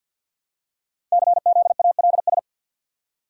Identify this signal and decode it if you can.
Morse code, heard as a keyed tone, about 53 wpm, 700 Hz. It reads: CZALS
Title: Code Groups - 5 Letters - 50wpm